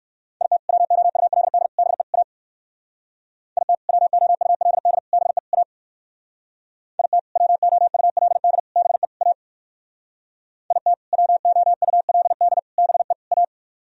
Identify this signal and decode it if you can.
Morse code, heard as a keyed tone, about 57 wpm, 700 Hz. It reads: IT WOULD BE A IT WOULD BE A IT WOULD BE A IT WOULD BE A